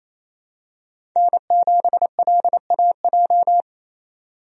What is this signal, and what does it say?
Morse code, heard as a keyed tone, about 28 wpm, 700 Hz. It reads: N7LAJ
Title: USA Call Signs - 5 Characters - 28wpm